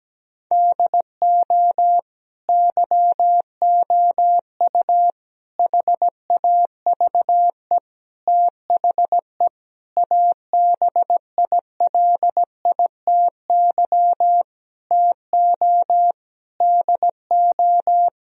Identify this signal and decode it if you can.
Morse code, heard as a keyed tone, about 17 wpm, 700 Hz. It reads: DO YOU HAVE THE ABILITY TO DO
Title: Sentences from Top 1000 Words - Review Part 3 - 17wpm